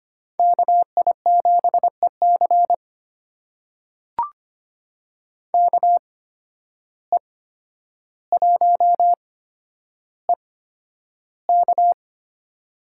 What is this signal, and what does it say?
Morse code, heard as a keyed tone, about 25 wpm, spaced as for 11 wpm, 700 Hz. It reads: KI7EC KE1EK
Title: USA Call Signs - 5 Characters - 8x Character Spacing - 25wpm